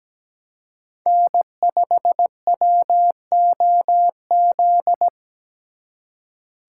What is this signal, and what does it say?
Morse code, heard as a keyed tone, about 21 wpm, 700 Hz. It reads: N5WOZ
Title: Speed Racing - USA 1x3 Call Signs 17wpm